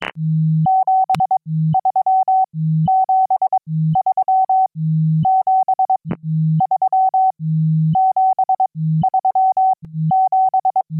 Morse code, heard as a keyed tone, about 22 wpm, spaced as for 13 wpm, 760 Hz. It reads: 73 73 73 73 7
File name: QSO-PA3FXB-DJ5AR